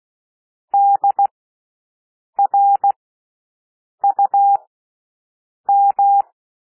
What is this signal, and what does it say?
Morse code, heard as a keyed tone, about 16 wpm, spaced as for 6 wpm, 800 Hz. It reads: DRUM
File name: cwmp3 / W1 SOUND WORDS 16 wpm